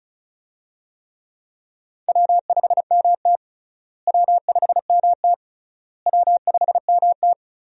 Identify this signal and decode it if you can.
Morse code, heard as a keyed tone, about 34 wpm, 700 Hz. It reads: W5MT W5MT W5MT
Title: USA Call Signs - 4 Characters - Repeated 3x - 35wpm